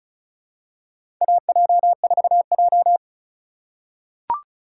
Morse code, heard as a keyed tone, about 34 wpm, 700 Hz. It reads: AJ4J
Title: USA Call Signs - 4 Characters - 3x Character Spacing - 35wpm